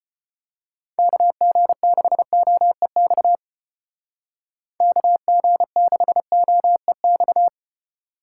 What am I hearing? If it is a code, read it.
Morse code, heard as a keyed tone, about 31 wpm, 700 Hz. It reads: KG6OEX KG6OEX
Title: Speed Racing - USA 2x3 Call Signs 25wpm